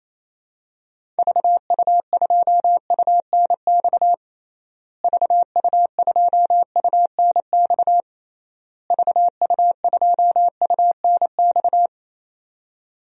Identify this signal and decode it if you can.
Morse code, heard as a keyed tone, about 27 wpm, 700 Hz. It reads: VU2UNX VU2UNX VU2UNX
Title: DXCC-100 Call Signs - 6 Characters - Repeated 3x - 28wpm